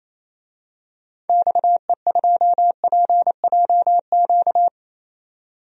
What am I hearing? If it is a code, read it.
Morse code, heard as a keyed tone, about 28 wpm, 700 Hz. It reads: XE2PJQ
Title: DXCC-100 Call Signs - 6 Characters - 28wpm